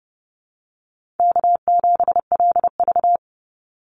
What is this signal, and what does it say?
Morse code, heard as a keyed tone, about 30 wpm, 700 Hz. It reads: K7LV
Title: Easy Call Signs Repeated 2x - 30wpm